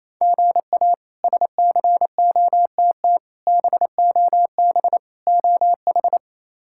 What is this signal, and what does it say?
Morse code, heard as a keyed tone, about 28 wpm, 700 Hz. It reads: GA SCOTT BOB OH
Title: K1USN SST QSO Practice - 28wpm